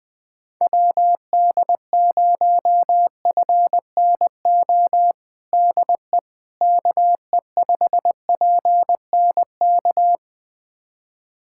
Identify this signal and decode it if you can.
Morse code, heard as a keyed tone, about 20 wpm, 700 Hz. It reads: WD0FNO DE KE5PNK
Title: USA Callsign DE Callsign - 6 Characters - 20wpm